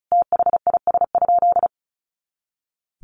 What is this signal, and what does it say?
Morse code, heard as a keyed tone, about 34 wpm, 700 Hz. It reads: THIS?
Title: Sentences from Top 200 Words 35wpm